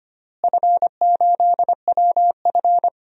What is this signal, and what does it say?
Morse code, heard as a keyed tone, about 25 wpm, 700 Hz. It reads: F8WF
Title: DXCC-100 Call Signs - 4 Characters - Repeated 2x - 25wpm